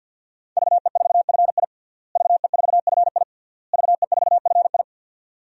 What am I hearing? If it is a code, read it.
Morse code, heard as a keyed tone, about 52 wpm, 700 Hz. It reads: VE4FI VE4FI VE4FI
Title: DXCC-100 Call Signs - 5 Characters - Repeated 3x - 50wpm